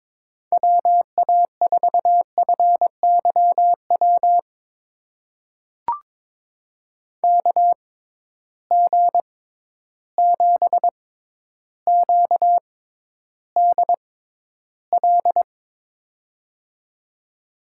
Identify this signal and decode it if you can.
Morse code, heard as a keyed tone, about 22 wpm, spaced as for 11 wpm, 700 Hz. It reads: WA4FYW KG7QDL
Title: USA Call Signs - 6 Characters - 6x Character Spacing - 22wpm